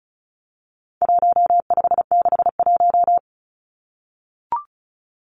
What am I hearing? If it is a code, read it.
Morse code, heard as a keyed tone, about 34 wpm, 700 Hz. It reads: 1561
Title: Numbers - 4-Digit 35wpm